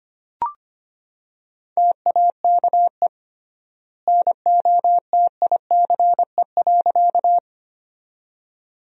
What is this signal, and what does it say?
Morse code, heard as a keyed tone, about 25 wpm, 700 Hz. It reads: TAKE NOTICE.
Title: 3x Word-Spacing - Sentences from Top 400 Words - 25wpm